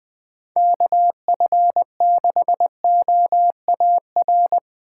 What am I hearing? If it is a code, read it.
Morse code, heard as a keyed tone, about 23 wpm, 700 Hz. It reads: KF6OAR
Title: Speed Racing - USA 2x3 Call Signs 20wpm